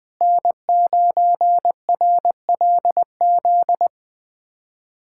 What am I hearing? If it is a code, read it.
Morse code, heard as a keyed tone, about 20 wpm, 700 Hz. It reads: N9RLZ
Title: USA Callsign DE Callsign - 5 Characters - 20wpm